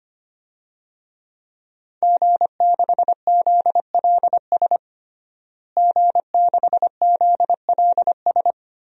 Morse code, heard as a keyed tone, about 25 wpm, 700 Hz. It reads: G6ZLS G6ZLS
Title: DXCC-100 Call Signs - 5 Characters - Repeated 3x - 25wpm